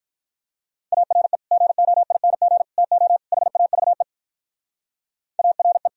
Morse code, heard as a keyed tone, about 44 wpm, 700 Hz. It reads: ARE GOING TO HAVE ARE